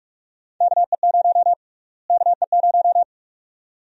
Morse code, heard as a keyed tone, about 46 wpm, 700 Hz. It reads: KE0 KE0
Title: USA Call Sign Prefixes - Repeated 2x - 45wpm